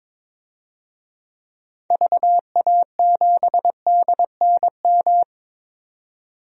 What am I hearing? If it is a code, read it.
Morse code, heard as a keyed tone, about 21 wpm, 700 Hz. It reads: VA7DNM
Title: DXCC-100 Call Signs - 6 Characters - 22wpm